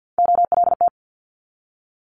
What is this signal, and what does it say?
Morse code, heard as a keyed tone, about 50 wpm, 700 Hz. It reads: OLT
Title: Multi-letter Phonemes 50wpm